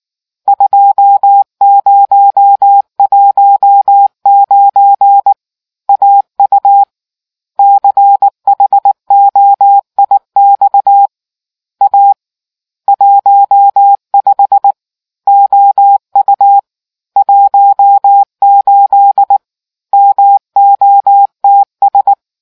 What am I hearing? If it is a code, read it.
Morse code, heard as a keyed tone, about 19 wpm, 800 Hz. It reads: 2019 AU CHOIX A 15 OU 18 MOTS